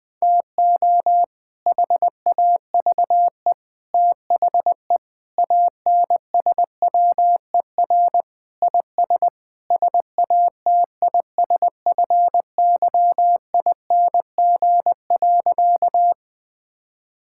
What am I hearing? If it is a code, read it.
Morse code, heard as a keyed tone, about 20 wpm, 700 Hz. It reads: TO HAVE THE ANSWER IS SATISFYING.
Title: Sentences with 3-Word Phrases - 20wpm